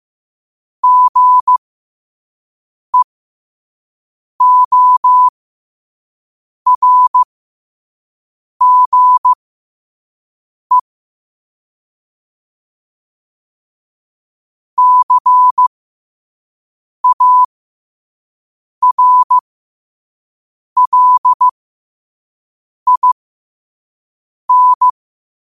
Morse code, heard as a keyed tone, about 15 wpm, spaced as for 5 wpm, 1000 Hz. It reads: GEORGE CARLIN